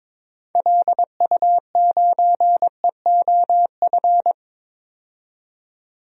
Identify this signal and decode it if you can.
Morse code, heard as a keyed tone, about 22 wpm, 700 Hz. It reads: LU9EOF
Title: DXCC-100 Call Signs - 6 Characters - 22wpm